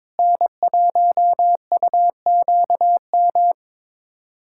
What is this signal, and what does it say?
Morse code, heard as a keyed tone, about 22 wpm, 700 Hz. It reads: N1UQM
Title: USA Callsign DE Callsign - 5 Characters - 22wpm